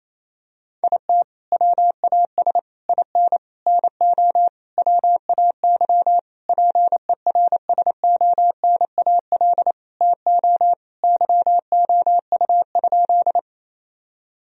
Morse code, heard as a keyed tone, about 28 wpm, 700 Hz. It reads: IT WAS IN NO WAY PERSONAL TO YOU?
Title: Sentences from Top 1000 Words - Review Part 4 - 28wpm